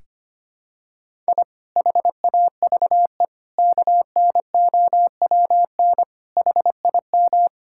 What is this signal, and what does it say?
Morse code, heard as a keyed tone, about 25 wpm, 700 Hz. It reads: I HAVE KNOWN HIM
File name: practice / 25wpm / Sentences with 4-Word Phrases - 25wpm